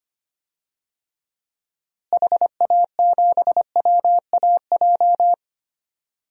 Morse code, heard as a keyed tone, about 25 wpm, 700 Hz. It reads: HA7WAJ
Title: DXCC-100 Call Signs - 6 Characters - Repeated 2x - 25wpm